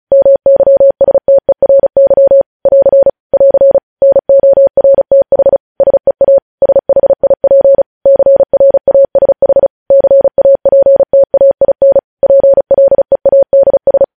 Morse code, heard as a keyed tone, about 36 wpm, 550 Hz. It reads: MYSTERY <AR> <AR> NORTH SEA SHIP CRASH CAPTAIN PLEADS
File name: podcast / News headlines-1x-35x35